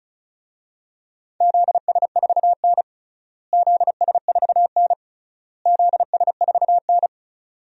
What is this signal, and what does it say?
Morse code, heard as a keyed tone, about 36 wpm, 700 Hz. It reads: ZS4N ZS4N ZS4N
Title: DXCC-100 Call Signs - 4 Characters - Repeated 3x - 35wpm